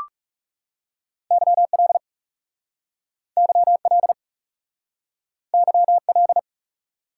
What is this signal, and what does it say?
Morse code, heard as a keyed tone, about 37 wpm, 700 Hz. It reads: YL YL YL
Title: Speed-Racing Top 180 QSO Elements 30wpm